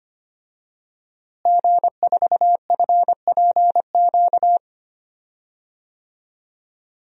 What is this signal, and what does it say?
Morse code, heard as a keyed tone, about 25 wpm, 700 Hz. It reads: G4FPQ
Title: DXCC-100 Call Signs - 5 Characters - Repeated 3x - 25wpm